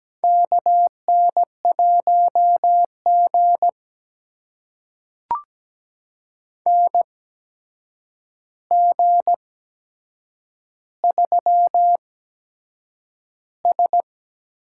Morse code, heard as a keyed tone, about 17 wpm, spaced as for 7 wpm, 700 Hz. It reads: KN1G NG3S
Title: USA Call Signs - 4 Characters - 8x Character Spacing - 17wpm